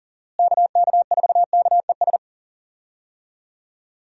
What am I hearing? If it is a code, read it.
Morse code, heard as a keyed tone, about 40 wpm, 700 Hz. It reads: KK4KES
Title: USA Call Signs - 6 Characters - Repeated 2x - 40wpm